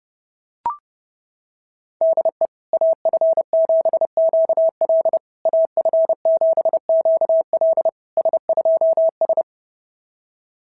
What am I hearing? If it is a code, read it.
Morse code, heard as a keyed tone, about 30 wpm, 690 Hz. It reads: DE AF7QL AF7QL S2S
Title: SOTA QSO Practice 30wpm